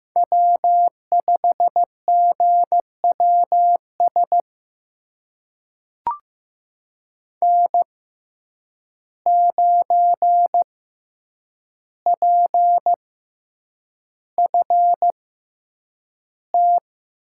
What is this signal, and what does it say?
Morse code, heard as a keyed tone, about 15 wpm, spaced as for 8 wpm, 700 Hz. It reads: W5GWS N9PFT
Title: USA Call Signs - 5 Characters - 6x Character Spacing - 15wpm